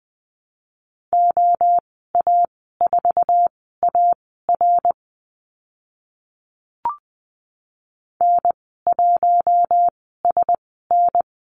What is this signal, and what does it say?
Morse code, heard as a keyed tone, about 20 wpm, spaced as for 15 wpm, 700 Hz. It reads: OA4AR N1SN
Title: 062 Lesson 11g - Letter h - Call Signs 20wpm